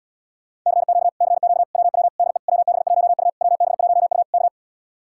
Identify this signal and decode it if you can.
Morse code, heard as a keyed tone, about 79 wpm, 700 Hz. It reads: CQ CQ WW DE WB1U WB1U K